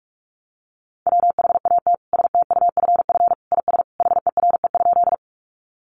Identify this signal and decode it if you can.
Morse code, heard as a keyed tone, about 44 wpm, 700 Hz. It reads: WHAT STUFF IS HERE?
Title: Sentences from Top 1400 Words 45wpm